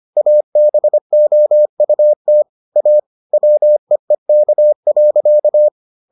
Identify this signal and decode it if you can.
Morse code, heard as a keyed tone, about 25 wpm, 600 Hz. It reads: ABOUT A WEEK.